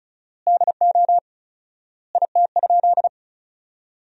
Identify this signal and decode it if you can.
Morse code, heard as a keyed tone, about 34 wpm, 700 Hz. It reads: DO IT?
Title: 4x Word-Spacing - Sentences from Top 100 Words - 35wpm